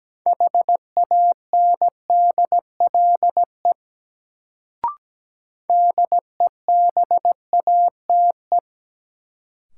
Morse code, heard as a keyed tone, about 17 wpm, 700 Hz. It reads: HANDLE DEBATE
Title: Sentences from Top 1100 Words - New Word Drill - 17wpm